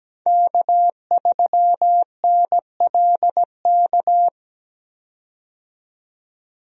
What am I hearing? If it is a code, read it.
Morse code, heard as a keyed tone, about 17 wpm, 700 Hz. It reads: K3NLK
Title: USA Call Signs - 5 Characters - 17wpm